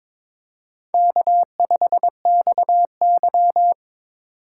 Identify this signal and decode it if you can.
Morse code, heard as a keyed tone, about 23 wpm, 700 Hz. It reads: K5XY
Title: USA Call Signs - 4 Characters - 4x Character Spacing - 22wpm